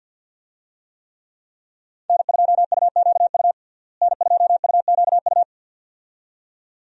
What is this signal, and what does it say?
Morse code, heard as a keyed tone, about 51 wpm, 700 Hz. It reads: N2UQU N2UQU
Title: USA Call Signs - 5 Characters - Repeated 2x - 50wpm